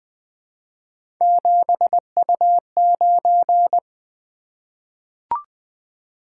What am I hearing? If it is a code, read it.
Morse code, heard as a keyed tone, about 20 wpm, 700 Hz. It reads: 7U9
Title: Three Letter-Number Combinations - 20wpm